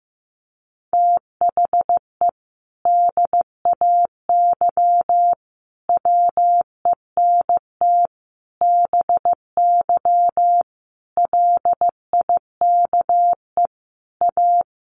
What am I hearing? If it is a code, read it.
Morse code, heard as a keyed tone, about 15 wpm, 700 Hz. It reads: THE DAY WENT BY LIKE A
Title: Sentences from Top 1500 Words 15wpm